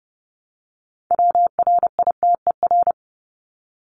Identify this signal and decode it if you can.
Morse code, heard as a keyed tone, about 30 wpm, 700 Hz. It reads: WRITER
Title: Top 1751-2000 Words 30wpm